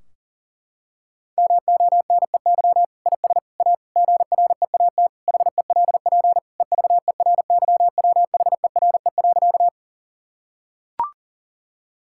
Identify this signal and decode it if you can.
Morse code, heard as a keyed tone, about 40 wpm, 700 Hz. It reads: MONEY IS A GREAT HELP EVERYWHERE.
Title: Sentences from Top 1600 Words 40wpm